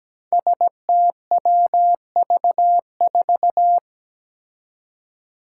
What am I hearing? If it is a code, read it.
Morse code, heard as a keyed tone, about 17 wpm, 700 Hz. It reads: STWV4
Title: Five Challenging Letter-Number Combinations 17wpm